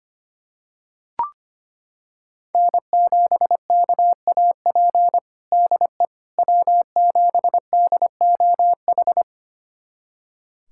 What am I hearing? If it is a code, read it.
Morse code, heard as a keyed tone, about 25 wpm, 700 Hz. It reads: N7KAP DE W7DOH
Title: USA Callsign DE Callsign - 5 Characters - 25wpm